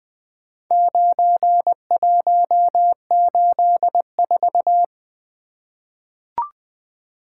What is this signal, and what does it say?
Morse code, heard as a keyed tone, about 21 wpm, 700 Hz. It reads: 9184
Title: Four Letter-Number Combinations - 20wpm